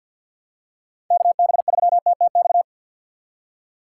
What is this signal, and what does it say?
Morse code, heard as a keyed tone, about 53 wpm, 700 Hz. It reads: KB3TTX
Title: USA Call Signs - 6 Characters - 8x Character Spacing - 50wpm